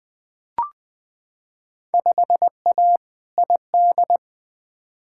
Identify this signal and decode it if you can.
Morse code, heard as a keyed tone, about 20 wpm, 700 Hz. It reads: 5A ID
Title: Field Day Exchanges Encore - 20wpm